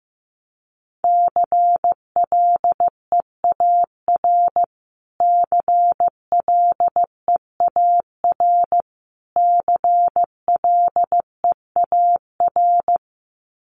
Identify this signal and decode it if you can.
Morse code, heard as a keyed tone, about 15 wpm, 700 Hz. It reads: CLEAR CLEAR CLEAR
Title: Top 500 Words Repeated 15wpm